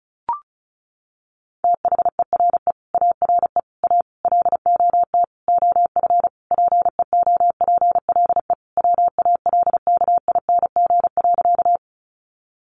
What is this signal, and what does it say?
Morse code, heard as a keyed tone, about 34 wpm, 700 Hz. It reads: THERE ARE A LOT OF PEOPLE WALKING.
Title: Easy Phrases 35wpm